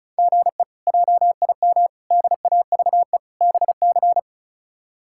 Morse code, heard as a keyed tone, about 36 wpm, 700 Hz. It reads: GE JIM DAVE BC
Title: K1USN SST QSO Practice - 35wpm